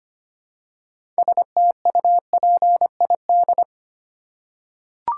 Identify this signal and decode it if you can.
Morse code, heard as a keyed tone, about 25 wpm, 700 Hz. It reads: STUPID
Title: Six Letter Words - 25wpm